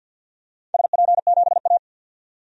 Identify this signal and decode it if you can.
Morse code, heard as a keyed tone, about 52 wpm, 700 Hz. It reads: SP7A
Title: DXCC-100 Call Signs - 4 Characters - 50wpm